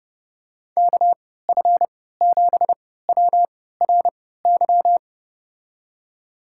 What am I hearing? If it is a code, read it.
Morse code, heard as a keyed tone, about 30 wpm, spaced as for 17 wpm, 700 Hz. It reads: KF7WRY
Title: USA Call Signs - 6 Characters - 3x Character Spacing - 30wpm